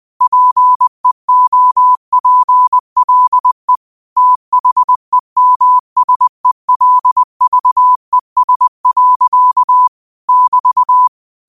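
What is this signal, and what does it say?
Morse code, heard as a keyed tone, about 20 wpm, 1000 Hz. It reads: PEOPLE THEMSELVES. <BT>
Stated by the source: 20 WPM Morse code quotes, QOTD at 20 WPM.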